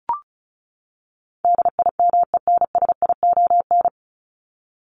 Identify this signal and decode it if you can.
Morse code, heard as a keyed tone, about 34 wpm, 700 Hz. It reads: DIMENSION
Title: Top 1501-1750 Words 35wpm